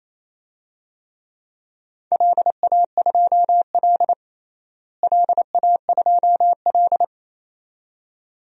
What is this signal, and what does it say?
Morse code, heard as a keyed tone, about 28 wpm, 700 Hz. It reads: LA2L LA2L
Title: DXCC-100 Call Signs - 4 Characters - Repeated 2x - 28wpm